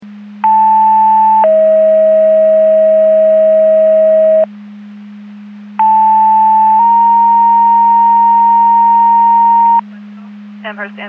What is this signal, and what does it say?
Two-tone page, heard as a one-second tone, then a three-second tone, about 880 then 640 Hz, about 880 then 930 Hz. It is heard from 0.4 to 9.8 s.
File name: Portage 154.385 Paging